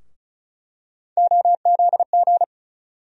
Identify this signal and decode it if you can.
Morse code, heard as a keyed tone, about 34 wpm, 700 Hz. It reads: OZG
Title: USA Call Sign Postfixes - 3 Letters 35wpm